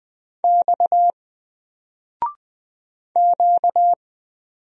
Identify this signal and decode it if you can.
Morse code, heard as a keyed tone, about 20 wpm, 700 Hz. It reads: X Q
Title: Single Letter-Number 20wpm